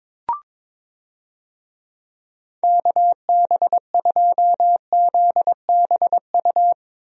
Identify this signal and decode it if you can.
Morse code, heard as a keyed tone, about 22 wpm, 700 Hz. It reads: KB2ZBU
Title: USA Call Signs - 6 Characters - Repeated 3x - 22wpm